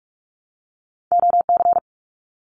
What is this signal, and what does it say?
Morse code, heard as a keyed tone, about 46 wpm, 700 Hz. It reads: OC
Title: Abbreviations Repeated 3x - 45wpm